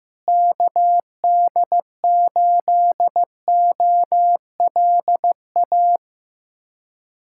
Morse code, heard as a keyed tone, about 15 wpm, 700 Hz. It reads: KD8OLA
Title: USA Callsign DE Callsign - 6 Characters - 15wpm